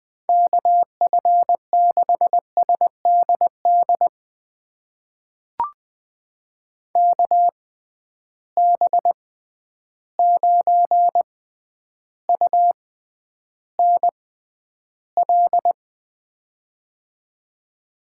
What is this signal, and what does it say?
Morse code, heard as a keyed tone, about 20 wpm, spaced as for 10 wpm, 700 Hz. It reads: KF6SDD KB9UNL
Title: USA Call Signs - 6 Characters - 6x Character Spacing - 20wpm